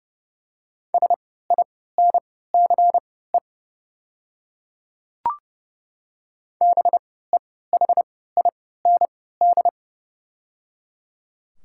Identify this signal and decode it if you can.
Morse code, heard as a keyed tone, about 30 wpm, spaced as for 17 wpm, 700 Hz. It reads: SINCE BEHIND
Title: 3x Character Spacing - Top 500 Words - 30wpm